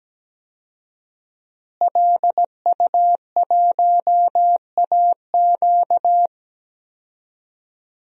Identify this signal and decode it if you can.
Morse code, heard as a keyed tone, about 17 wpm, 700 Hz. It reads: LU1AQ
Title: DXCC-100 Call Signs - 5 Characters - Repeated 2x - 17wpm